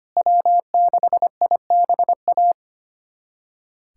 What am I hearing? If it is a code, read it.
Morse code, heard as a keyed tone, about 25 wpm, 700 Hz. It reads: W6IBA
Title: USA Callsign DE Callsign - 5 Characters - 25wpm